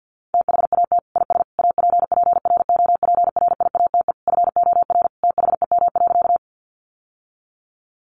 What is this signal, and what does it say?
Morse code, heard as a keyed tone, about 50 wpm, 700 Hz. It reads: THAT IS APPROPRIATE FOR THEM.
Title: Sentences from Top 900 Words 50wpm